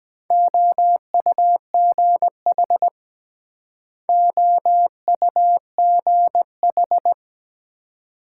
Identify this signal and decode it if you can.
Morse code, heard as a keyed tone, about 22 wpm, 700 Hz. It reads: OUGH OUGH